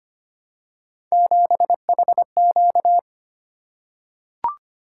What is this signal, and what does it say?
Morse code, heard as a keyed tone, about 25 wpm, 700 Hz. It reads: 7HQ
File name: Three Letter-Number Combinations - 25wpm